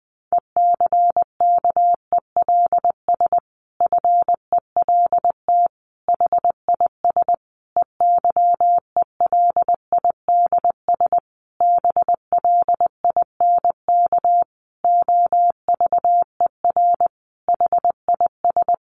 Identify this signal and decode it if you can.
Morse code, heard as a keyed tone, about 20 wpm, 700 Hz. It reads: ECKELS FELT HIS EYELIDS BLINK OVER HIS
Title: A Sound Of Thunder 20wpm